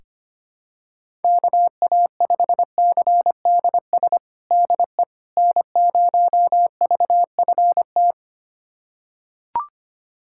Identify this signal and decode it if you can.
Morse code, heard as a keyed tone, about 25 wpm, 700 Hz. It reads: KA5CDS DE N0VFT
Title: USA Callsign DE Callsign 25wpm